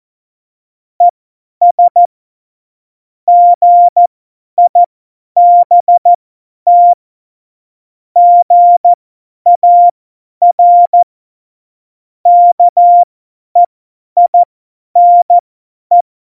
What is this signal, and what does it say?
Morse code, heard as a keyed tone, about 14 wpm, spaced as for 10 wpm, 700 Hz. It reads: ES GIBT GAR KEINE
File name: Morsekurs / Witze / Witz 24. November 2020, 50 F70